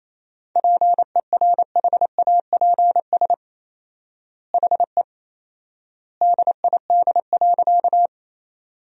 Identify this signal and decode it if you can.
Morse code, heard as a keyed tone, about 28 wpm, 700 Hz. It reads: PERHAPS HE DID.
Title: 4x Word-Spacing - Sentences from Top 500 Words - 28wpm